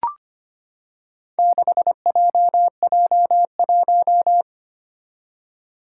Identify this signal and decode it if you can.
Morse code, heard as a keyed tone, about 25 wpm, 700 Hz. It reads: 6JJ1
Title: Four Challenging Letter-Number Combinations 25wpm